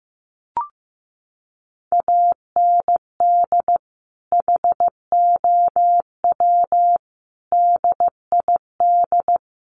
Morse code, heard as a keyed tone, about 15 wpm, 700 Hz. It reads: AND HOW DID
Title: Sentences from Top 100 Words 15wpm